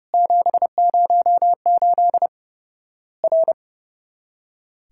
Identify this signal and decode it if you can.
Morse code, heard as a keyed tone, about 30 wpm, 700 Hz. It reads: 708 R
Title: Worked All Europe DX Contest - QTC Exchange - 30wpm